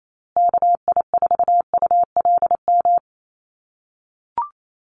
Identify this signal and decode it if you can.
Morse code, heard as a keyed tone, about 28 wpm, 700 Hz. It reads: KI4ULM
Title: USA Call Signs Repeated 3x 28wpm